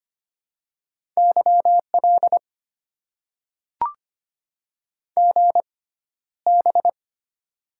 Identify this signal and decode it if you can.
Morse code, heard as a keyed tone, about 25 wpm, spaced as for 13 wpm, 700 Hz. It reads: YL GB
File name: Top 90 QSO Elements - 6x Character Spacing - 25wpm